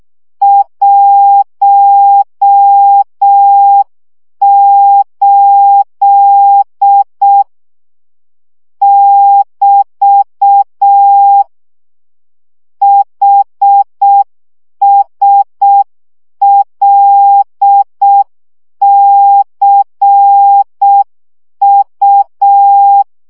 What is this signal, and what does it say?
Morse code, heard as a keyed tone, about 6 wpm, 800 Hz. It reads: 18 <BT> HSLCU
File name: morze vizsga 2011-3 5